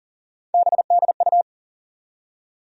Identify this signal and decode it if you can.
Morse code, heard as a keyed tone, about 40 wpm, 700 Hz. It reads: BDU
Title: USA Call Sign Postfixes - 3 Letters 40wpm